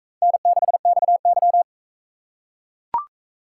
Morse code, heard as a keyed tone, about 38 wpm, 700 Hz. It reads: N6XY